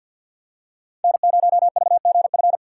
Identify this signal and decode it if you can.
Morse code, heard as a keyed tone, about 50 wpm, 700 Hz. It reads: N0VGF
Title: USA Call Signs - 5 Characters - 6x Character Spacing - 50wpm